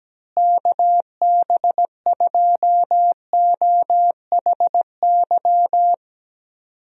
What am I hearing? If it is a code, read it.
Morse code, heard as a keyed tone, about 17 wpm, 700 Hz. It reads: KB2OHY
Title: USA Call Signs - 6 Characters - Repeated 3x - 17wpm